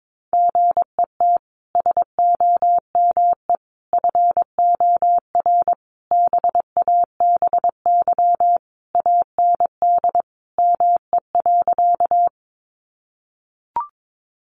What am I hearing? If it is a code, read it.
Morse code, heard as a keyed tone, about 22 wpm, 700 Hz. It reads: GET SOME FOR BABY AND ME.
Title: Sentences from Top 1300 Words 22wpm